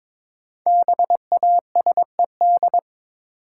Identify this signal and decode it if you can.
Morse code, heard as a keyed tone, about 22 wpm, 700 Hz. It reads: BASED
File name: Build A Phrase - 2-Word Phrases - 22wpm